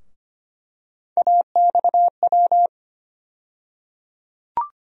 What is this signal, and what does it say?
Morse code, heard as a keyed tone, about 25 wpm, 700 Hz. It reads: AXW
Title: USA Call Sign Postfixes - 3 Letters 25wpm